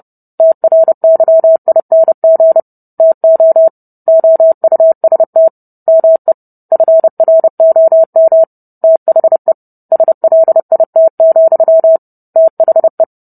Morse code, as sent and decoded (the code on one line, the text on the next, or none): - .-. -.-- .. -. --. / - --- / --- ..- ... - / -- . / ..-. .-. --- -- / - .... / ... .-.. .. - --..-- / - .... .
TRYING TO OUST ME FROM THE SLIT, THE